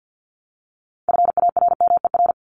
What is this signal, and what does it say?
Morse code, heard as a keyed tone, about 50 wpm, 700 Hz. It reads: FARMER
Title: Top 1751-2000 Words 50wpm